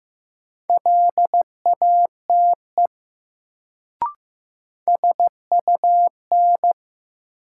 Morse code, heard as a keyed tone, about 15 wpm, 700 Hz. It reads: LATE SUN
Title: Sentences from Top 300 Words - New Word Drill - 15wpm